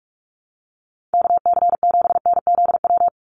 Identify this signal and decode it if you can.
Morse code, heard as a keyed tone, about 46 wpm, 700 Hz. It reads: KC7NZW
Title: USA Call Signs 45wpm